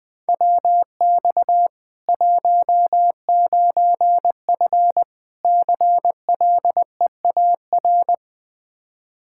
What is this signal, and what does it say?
Morse code, heard as a keyed tone, about 20 wpm, 700 Hz. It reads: WX 19F CLEAR
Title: Practice QSO - Exchange 2 - Info - 20wpm